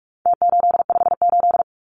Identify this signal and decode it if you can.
Morse code, heard as a keyed tone, about 44 wpm, 700 Hz. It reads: T858
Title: Numbers - Contest Style - 4-Digit 45wpm